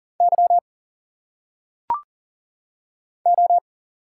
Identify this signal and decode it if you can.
Morse code, heard as a keyed tone, about 40 wpm, 700 Hz. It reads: Y O
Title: Single Letters - Phonetic Pronunciation 40wpm